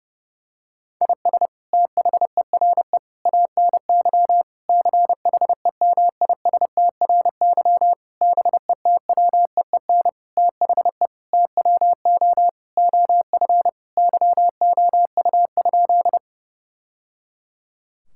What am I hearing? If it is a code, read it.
Morse code, heard as a keyed tone, about 30 wpm, 700 Hz. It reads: IS THERE ANY CHEMISTRY BETWEEN THE TWO OF YOU?
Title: Sentences from Top 1000 Words - Review Part 6 - 30wpm